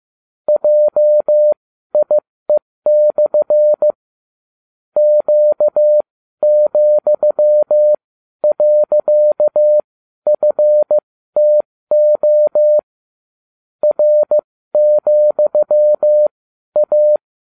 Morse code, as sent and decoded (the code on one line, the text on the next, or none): .--- .. -..-. / --.- --..-- .-.-.- ..-. - --- / .-. --..-- .-
JIE/ Q,.FTO R,A